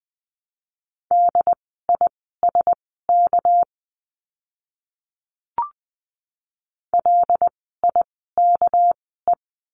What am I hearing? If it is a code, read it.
Morse code, heard as a keyed tone, about 20 wpm, spaced as for 14 wpm, 700 Hz. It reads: DISK LIKE
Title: 199 Lesson 32f - Letter k - Words 20wpm